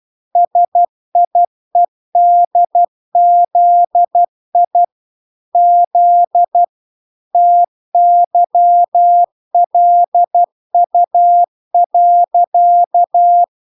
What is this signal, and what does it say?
Morse code, heard as a keyed tone, about 12 wpm, 700 Hz. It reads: SIEDZI Z TYLU.